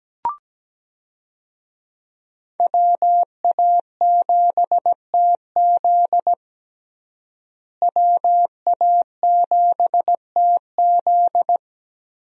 Morse code, heard as a keyed tone, about 17 wpm, 700 Hz. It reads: WA7TZ WA7TZ
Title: USA Call Signs - 5 Characters - Repeated 2x - 17wpm